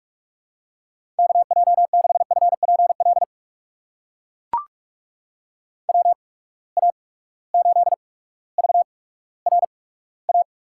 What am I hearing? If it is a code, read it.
Morse code, heard as a keyed tone, about 44 wpm, spaced as for 19 wpm, 700 Hz. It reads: KJ6FPL WA8VRA
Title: USA Call Signs - 6 Characters - 8x Character Spacing - 45wpm